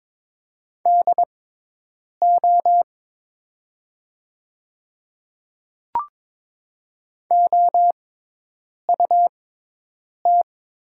Morse code, heard as a keyed tone, about 22 wpm, spaced as for 8 wpm, 690 Hz. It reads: DO OUT
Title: 6x Character Spacing - Top 100 Words - 22wpm